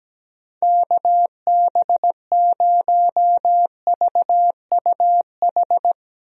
Morse code, heard as a keyed tone, about 17 wpm, 700 Hz. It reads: KB0VUH
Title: USA Call Signs - 6 Characters - 17wpm